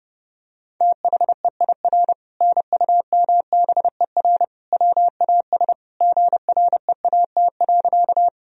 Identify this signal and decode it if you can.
Morse code, heard as a keyed tone, about 30 wpm, 700 Hz. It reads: THEIR NUMBER WAS GREAT.
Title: Sentences from Top 1000 Words - Review Part 1 - 30wpm